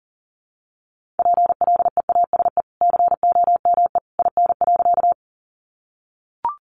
Morse code, heard as a keyed tone, about 40 wpm, 700 Hz. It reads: PLEASE COME IN.
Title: Sentences from Top 1500 Words 40wpm